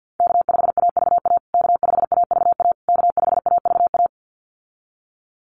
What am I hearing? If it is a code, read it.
Morse code, heard as a keyed tone, about 50 wpm, 700 Hz. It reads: K5AVA K5AVA K5AVA
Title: USA Call Signs Repeated 3x 50wpm